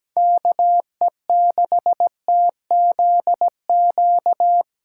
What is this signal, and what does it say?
Morse code, heard as a keyed tone, about 17 wpm, 700 Hz. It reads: KE6TZQ
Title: USA Callsign DE Callsign 17wpm